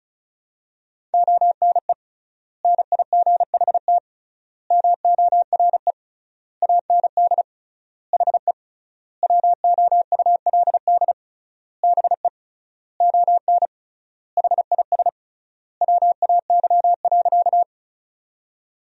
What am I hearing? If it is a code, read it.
Morse code, heard as a keyed tone, about 36 wpm, 700 Hz. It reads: ONE NIGHT MORE AND HE WOULD BE ON HIS WAY.
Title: 3x Word-Spacing - Sentences from Top 300 Words - 35wpm